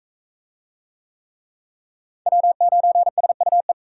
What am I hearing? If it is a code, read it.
Morse code, heard as a keyed tone, about 44 wpm, 700 Hz. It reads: W9SUE